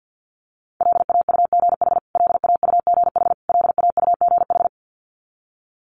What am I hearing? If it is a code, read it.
Morse code, heard as a keyed tone, about 50 wpm, 700 Hz. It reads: LAUGH LAUGH LAUGH
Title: Top 500 Words Repeated 50wpm